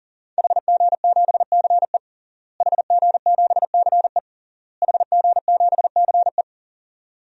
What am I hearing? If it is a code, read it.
Morse code, heard as a keyed tone, about 37 wpm, 700 Hz. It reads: HG7CE HG7CE HG7CE
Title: DXCC-100 Call Signs - 5 Characters - Repeated 3x - 40wpm